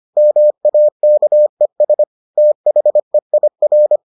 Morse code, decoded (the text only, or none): MAKES THEIR